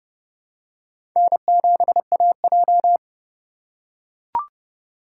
Morse code, heard as a keyed tone, about 31 wpm, 700 Hz. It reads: N7AJ
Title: USA Call Signs - 3x Character Spacing - 30wpm